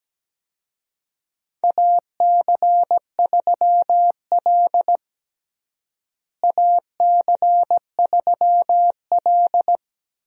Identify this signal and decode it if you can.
Morse code, heard as a keyed tone, about 17 wpm, 700 Hz. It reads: AC3L AC3L
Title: USA Call Signs - 4 Characters - Repeated 3x - 17wpm